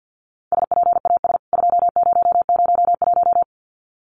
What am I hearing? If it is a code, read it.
Morse code, heard as a keyed tone, about 50 wpm, 700 Hz. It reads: SPAS 2001
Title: CWT Exchange 50wpm